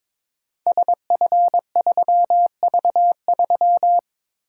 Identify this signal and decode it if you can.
Morse code, heard as a keyed tone, about 22 wpm, 700 Hz. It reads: SF3V3
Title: Code Groups - 5 Letter:Numbers - 22wpm